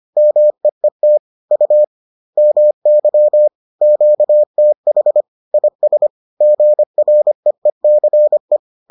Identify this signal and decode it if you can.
Morse code, heard as a keyed tone, about 25 wpm, 600 Hz. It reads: MEET U MY QTH IS GREECE